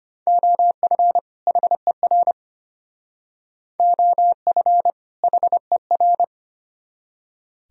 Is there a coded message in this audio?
Yes: Morse code from 0.3 to 6.2 s, heard as a keyed tone, about 32 wpm, 700 Hz. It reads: OF HER OF HER